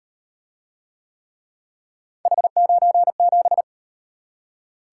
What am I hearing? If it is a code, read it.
Morse code, heard as a keyed tone, about 36 wpm, 700 Hz. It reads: H97